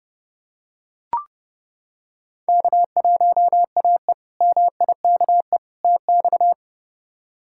Morse code, heard as a keyed tone, about 30 wpm, 700 Hz. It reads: K1AE MIKE TX
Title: K1USN SST QSO Practice - 30wpm